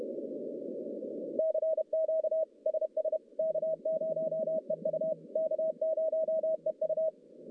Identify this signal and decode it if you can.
Morse code, heard as a keyed tone, about 33 wpm, 640 Hz. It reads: CQ SS K0EU K0EU
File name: contests / CW-SS-k5tr- -1200